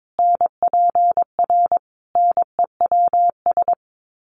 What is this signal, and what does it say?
Morse code, heard as a keyed tone, about 22 wpm, 700 Hz. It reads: NPR NEWS
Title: Sets of 5 Words Encore 22wpm